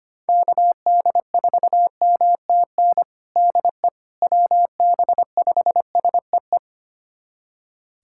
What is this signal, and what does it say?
Morse code, heard as a keyed tone, about 25 wpm, 700 Hz. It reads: KD4MTN DE WB5SEE
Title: USA Callsign DE Callsign - 6 Characters - 25wpm